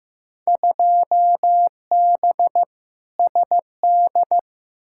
Morse code, heard as a keyed tone, about 15 wpm, 700 Hz. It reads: 2B SD
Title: Field Day Exchanges Encore - 15wpm